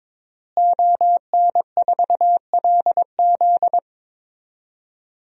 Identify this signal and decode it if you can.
Morse code, heard as a keyed tone, about 22 wpm, 700 Hz. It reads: ON4LZ
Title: DXCC-100 Call Signs - 5 Characters - Repeated 3x - 22wpm